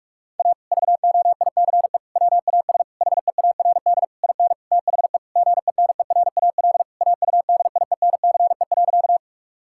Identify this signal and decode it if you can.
Morse code, heard as a keyed tone, about 46 wpm, 700 Hz. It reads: A VOICE WAS HEARD IN THE GENERAL AUDIENCE.
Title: Sentences from Top 1000 Words - Review Part 5 - 45wpm